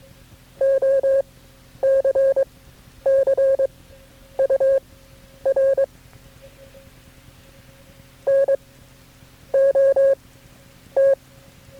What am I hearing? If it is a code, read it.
Morse code, heard as a keyed tone, about 22 wpm, spaced as for 9 wpm, 540 Hz. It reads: OCCUR NOT
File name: KB6MT-tape3-side1